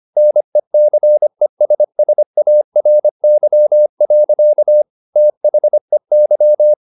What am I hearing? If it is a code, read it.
Morse code, heard as a keyed tone, about 25 wpm, 600 Hz. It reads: NECESSARY. THEY
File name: the old man and the sea part 1 25wpm